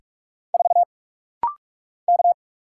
Morse code, heard as a keyed tone, about 46 wpm, 700 Hz. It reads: 4 K
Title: Single Letter-Number Alt - Rapid-Fire 45wpm